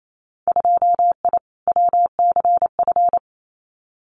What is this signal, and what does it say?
Morse code, heard as a keyed tone, about 28 wpm, 700 Hz. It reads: 2I WCF
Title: Winter Field Day Exchanges 28wpm